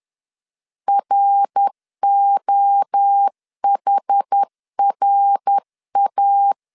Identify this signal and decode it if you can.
Morse code, heard as a keyed tone, about 10 wpm, 800 Hz. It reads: ROHRA
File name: 10wpm